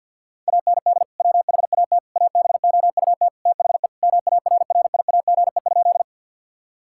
Morse code, heard as a keyed tone, about 50 wpm, 700 Hz. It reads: AND WHAT ABOUT THE MARRIAGE?
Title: Sentences from Top 1000 Words - Review Part 5 - 50wpm